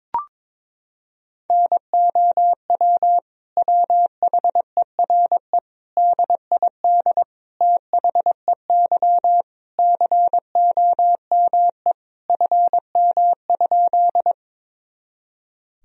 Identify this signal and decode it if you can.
Morse code, heard as a keyed tone, about 21 wpm, 700 Hz. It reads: NOW WHERE DID THEY COME FM?
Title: Sentences from Top 200 Words and Abbreviations 22wpm